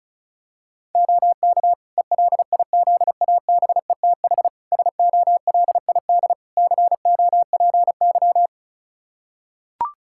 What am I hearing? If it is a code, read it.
Morse code, heard as a keyed tone, about 34 wpm, 700 Hz. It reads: OK ELIZABETH SOLID COPY
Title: Practice QSO - Exchange 3 - More Info - 35wpm